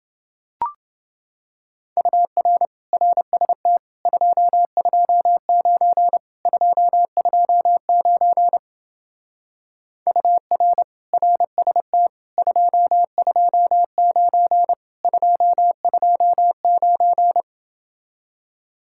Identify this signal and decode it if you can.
Morse code, heard as a keyed tone, about 28 wpm, 700 Hz. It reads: UR RST 229 229 UR RST 229 229